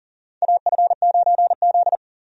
Morse code, heard as a keyed tone, about 40 wpm, 700 Hz. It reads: AF9Z
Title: USA Call Signs - 4 Characters - 40wpm